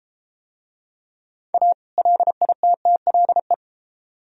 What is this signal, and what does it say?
Morse code, heard as a keyed tone, about 32 wpm, 700 Hz. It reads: A LITTLE